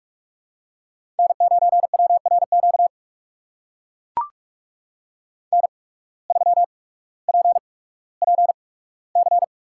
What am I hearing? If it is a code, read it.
Morse code, heard as a keyed tone, about 44 wpm, spaced as for 19 wpm, 700 Hz. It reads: N9WRQ N3PPC
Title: USA Call Signs - 5 Characters - 8x Character Spacing - 45wpm